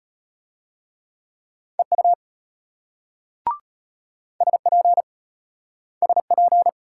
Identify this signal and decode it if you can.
Morse code, heard as a keyed tone, about 33 wpm, 700 Hz. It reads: EU SP SP